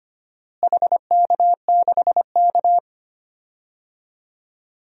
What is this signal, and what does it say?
Morse code, heard as a keyed tone, about 25 wpm, 700 Hz. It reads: HK6K
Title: DXCC-100 Call Signs - 4 Characters - Repeated 2x - 25wpm